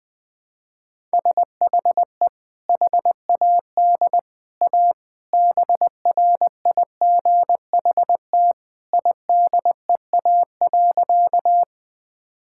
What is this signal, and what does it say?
Morse code, heard as a keyed tone, about 20 wpm, 700 Hz. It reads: SHE HAD A BRIGHT IDEA.
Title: Sentences with 2-Word Phrases - Spoken Beginning - 20wpm